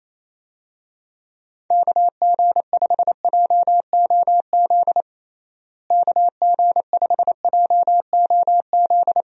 Morse code, heard as a keyed tone, about 28 wpm, 700 Hz. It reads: KG5JOZ KG5JOZ
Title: USA Call Signs - 6 Characters - Repeated 2x - 28wpm